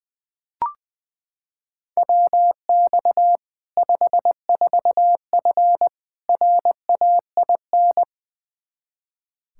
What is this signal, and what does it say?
Morse code, heard as a keyed tone, about 20 wpm, 700 Hz. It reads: WX 54F RAIN
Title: Practice QSO - Exchange 2 - Info - 20wpm